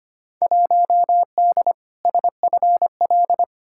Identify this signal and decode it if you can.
Morse code, heard as a keyed tone, about 25 wpm, 700 Hz. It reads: 1D SFL
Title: Field Day Exchanges - 25wpm